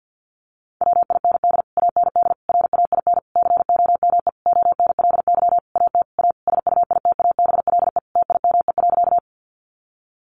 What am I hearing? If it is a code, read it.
Morse code, heard as a keyed tone, about 50 wpm, 700 Hz. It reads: WIND AND RAIN COME ONLY AT A SUITABLE TIME.
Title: Sentences from Top 1400 Words 50wpm